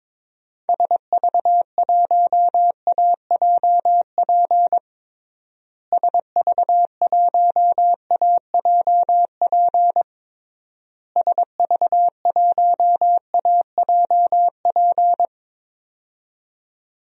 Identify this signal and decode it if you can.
Morse code, heard as a keyed tone, about 22 wpm, 700 Hz. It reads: SV1AJP SV1AJP SV1AJP
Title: DXCC-100 Call Signs - 6 Characters - Repeated 3x - 22wpm